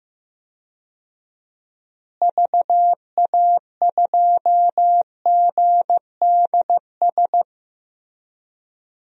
Morse code, heard as a keyed tone, about 15 wpm, 700 Hz. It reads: VA2GDS
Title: DXCC-100 Call Signs - 6 Characters - Repeated 2x - 15wpm